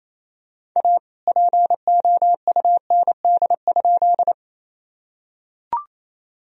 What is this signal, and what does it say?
Morse code, heard as a keyed tone, about 27 wpm, 700 Hz. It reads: A POUND?
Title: Sentences from Top 1000 Words - Review Part 2 - 28wpm